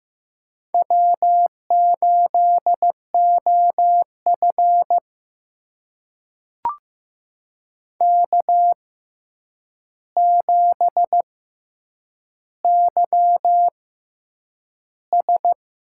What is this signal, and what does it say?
Morse code, heard as a keyed tone, about 15 wpm, spaced as for 8 wpm, 700 Hz. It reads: W8OF K7YS
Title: USA Call Signs - 4 Characters - 6x Character Spacing - 15wpm